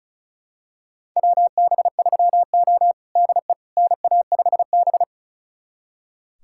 Morse code, heard as a keyed tone, about 36 wpm, 700 Hz. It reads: WB3O DE NA5B
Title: USA Callsign DE Callsign - 4 Characters - 35wpm